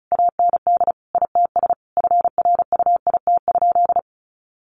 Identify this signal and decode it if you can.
Morse code, heard as a keyed tone, about 34 wpm, 700 Hz. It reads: AND ITS FRUIT?
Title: Sentences from Top 1500 Words 35wpm